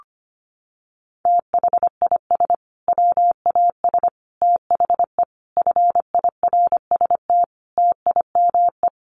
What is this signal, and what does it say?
Morse code, heard as a keyed tone, about 25 wpm, 700 Hz. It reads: THIS WAS THE FIRST TIME
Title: Sets of 5 Words Encore 25wpm